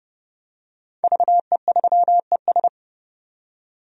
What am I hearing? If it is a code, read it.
Morse code, heard as a keyed tone, about 29 wpm, 700 Hz. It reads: VE3ES
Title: DXCC-100 Call Signs - 5 Characters - 30wpm